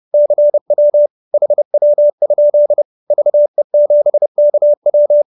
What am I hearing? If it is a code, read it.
Morse code, heard as a keyed tone, about 30 wpm, 590 Hz. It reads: CW HW? VE7KW
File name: QSO314 30